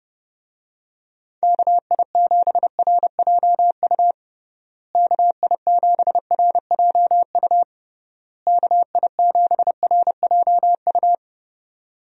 Morse code, heard as a keyed tone, about 30 wpm, 700 Hz. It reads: KI7RJU KI7RJU KI7RJU
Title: USA Call Signs - 6 Characters - Repeated 3x - 30wpm